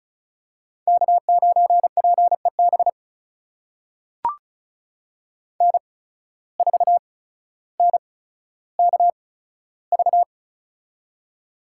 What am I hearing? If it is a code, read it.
Morse code, heard as a keyed tone, about 36 wpm, spaced as for 15 wpm, 700 Hz. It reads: K9PEB N4NKV
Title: USA Call Signs - 5 Characters - 8x Character Spacing - 35wpm